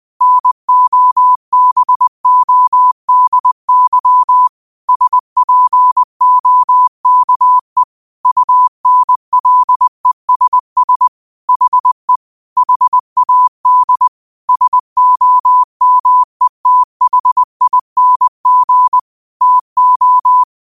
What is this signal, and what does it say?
Morse code, heard as a keyed tone, about 20 wpm, 1000 Hz. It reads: NOBODY SPOKE UNLESS HE HAD SOMETHING TO
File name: qotdfiles / 20wpm